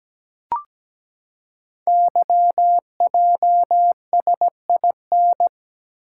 Morse code, heard as a keyed tone, about 17 wpm, 700 Hz. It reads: YJSIN
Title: Code Groups - 5 Letters - 17wpm